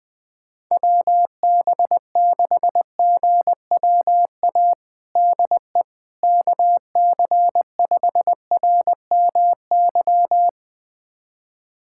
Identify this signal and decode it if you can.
Morse code, heard as a keyed tone, about 20 wpm, 700 Hz. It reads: WB6GWA DE KC5RMY
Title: USA Callsign DE Callsign 20wpm